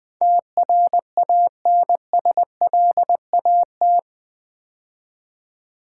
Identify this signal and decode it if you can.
Morse code, heard as a keyed tone, about 20 wpm, 700 Hz. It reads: TRANSLAT
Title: Build A Word - 4 Letters At A Time - 11-Letter Words - 20wpm